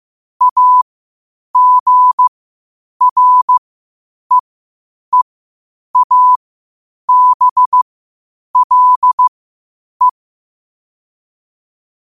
Morse code, heard as a keyed tone, about 15 wpm, spaced as for 8 wpm, 1000 Hz. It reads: AGREEABLE
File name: qotdfiles / 8wpm